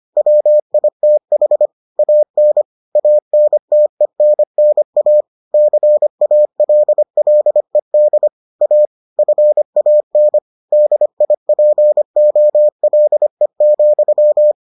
Morse code, heard as a keyed tone, about 25 wpm, 600 Hz. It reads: WITH AN ANTENNA CALLED A FAN DIPOLE,